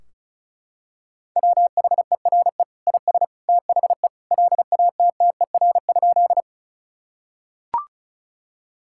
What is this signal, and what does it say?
Morse code, heard as a keyed tone, about 36 wpm, 700 Hz. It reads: WHERE IS THE LATTER?
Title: Sentences from Top 1600 Words 35wpm